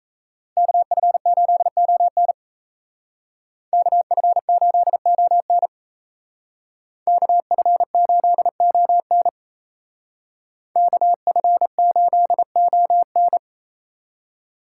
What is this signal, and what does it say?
Morse code, heard as a keyed tone, about 36 wpm, 700 Hz. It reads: KF8ON KF8ON KF8ON KF8ON